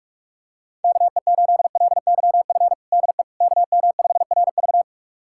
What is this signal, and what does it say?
Morse code, heard as a keyed tone, about 41 wpm, 700 Hz. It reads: KE8LYF DE KM5RV
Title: USA Callsign DE Callsign 45wpm